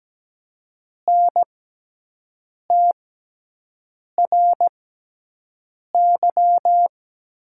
Morse code, heard as a keyed tone, about 17 wpm, spaced as for 6 wpm, 700 Hz. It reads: NTRY
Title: 6x Character Spacing - Top 300 Words - 17wpm